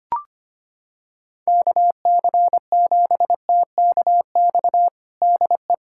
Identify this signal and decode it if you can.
Morse code, heard as a keyed tone, about 25 wpm, 700 Hz. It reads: KC7TKX DE
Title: USA Callsign DE Callsign - 6 Characters - 25wpm